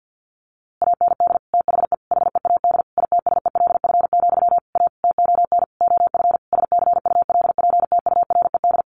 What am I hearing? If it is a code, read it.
Morse code, heard as a keyed tone, about 50 wpm, 700 Hz. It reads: AND THE HEAD ITSELF, A TON OF SCULPTURED
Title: A Sound Of Thunder 50wpm